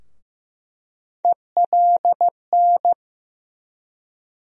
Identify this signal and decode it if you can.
Morse code, heard as a keyed tone, about 15 wpm, 700 Hz. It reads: ELN
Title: USA Call Sign Postfixes - 3 Letters 15wpm